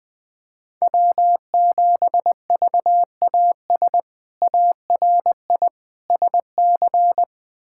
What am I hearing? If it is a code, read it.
Morse code, heard as a keyed tone, about 20 wpm, 700 Hz. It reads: W7VAS ARI SC
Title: K1USN SST QSO Practice - 20wpm